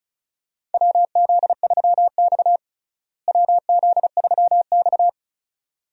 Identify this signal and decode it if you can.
Morse code, heard as a keyed tone, about 34 wpm, 700 Hz. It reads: WZ3X WZ3X
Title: USA Call Signs - 4 Characters - Repeated 2x - 35wpm